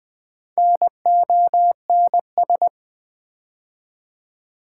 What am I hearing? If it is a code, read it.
Morse code, heard as a keyed tone, about 20 wpm, 690 Hz. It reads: NONS
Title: Build A Word - 2 Letters At A Time - 8-Letter Words - 20wpm